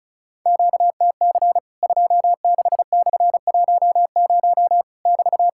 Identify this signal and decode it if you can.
Morse code, heard as a keyed tone, about 36 wpm, 700 Hz. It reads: QTC 26/10 <BT>
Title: Worked All Europe DX Contest - QTC Exchange - 35wpm